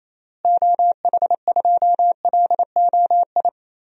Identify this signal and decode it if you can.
Morse code, heard as a keyed tone, about 28 wpm, 700 Hz. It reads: OH2LOI
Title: DXCC-100 Call Signs - 6 Characters - Repeated 3x - 28wpm